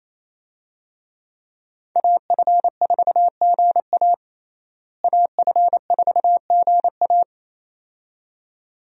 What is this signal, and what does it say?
Morse code, heard as a keyed tone, about 27 wpm, 700 Hz. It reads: AF4GA AF4GA
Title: USA Call Signs - 5 Characters - Repeated 2x - 28wpm